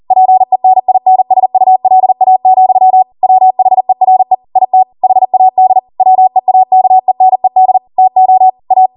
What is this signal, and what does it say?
Morse code, heard as a keyed tone, about 40 wpm, 750 Hz. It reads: PENINSULA, WHERE IT HAD WEAKENED TO A